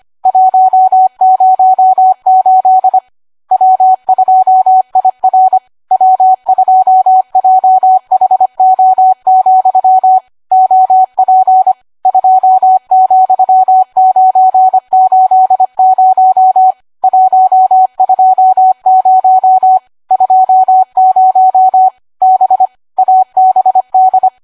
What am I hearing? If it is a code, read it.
Morse code, heard as a keyed tone, about 25 wpm, 750 Hz, with radio noise. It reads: 108 W2IR W2JHO, OP 2,980 120 20 B ABD